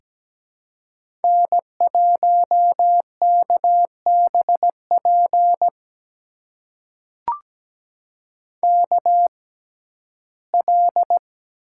Morse code, heard as a keyed tone, about 17 wpm, spaced as for 12 wpm, 700 Hz. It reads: N1KBP KL
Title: USA Call Signs - 5 Characters - 6x Character Spacing - 17wpm